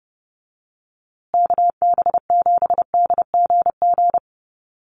Morse code, heard as a keyed tone, about 30 wpm, 700 Hz. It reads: KB7DGG
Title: USA Call Signs Repeated 2x 30wpm